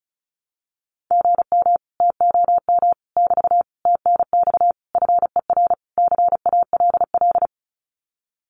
Morse code, heard as a keyed tone, about 34 wpm, 700 Hz. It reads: GM TOM <BT> TNX FER CALL
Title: Common QSO Phrases Encore 35wpm